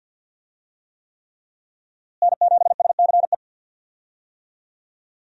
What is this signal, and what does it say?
Morse code, heard as a keyed tone, about 47 wpm, 700 Hz. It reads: N7SCE
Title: USA Call Signs - 5 Characters - Repeated 3x - 50wpm